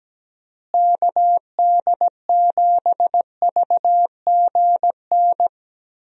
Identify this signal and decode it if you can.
Morse code, heard as a keyed tone, about 17 wpm, 700 Hz. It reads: KD7VGN
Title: USA Call Signs - 6 Characters - Repeated 2x - 17wpm